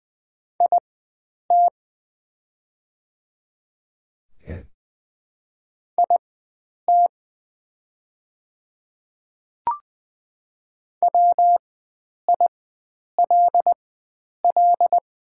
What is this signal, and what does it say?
Morse code, heard as a keyed tone, about 20 wpm, spaced as for 9 wpm, 700 Hz. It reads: IT IT WILL
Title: 4x Character Spacing - Top 100 Words - 20wpm